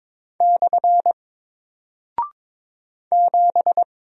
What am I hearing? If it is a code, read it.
Morse code, heard as a keyed tone, about 22 wpm, 700 Hz. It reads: / 7
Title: Single Character (Letter:Number:Punctuation:Prosign) Alt 22wpm